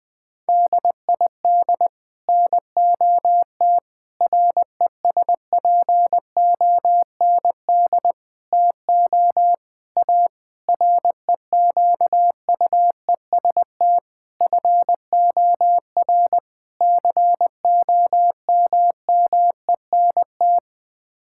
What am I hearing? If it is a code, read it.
Morse code, heard as a keyed tone, about 19 wpm, 700 Hz. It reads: DID NOT RESPOND TO A REQUEST FOR COMMENT
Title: Sets of 8 Words - 20wpm